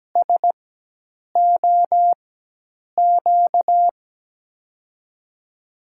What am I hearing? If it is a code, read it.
Morse code, heard as a keyed tone, about 17 wpm, spaced as for 8 wpm, 700 Hz. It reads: SOQ
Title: USA Call Signs - 5 Characters - 4x Character Spacing - 17wpm